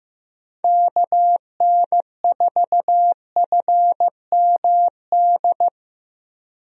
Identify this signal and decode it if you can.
Morse code, heard as a keyed tone, about 15 wpm, 700 Hz. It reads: KN4FMD
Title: USA Call Signs - 6 Characters - Repeated 3x - 15wpm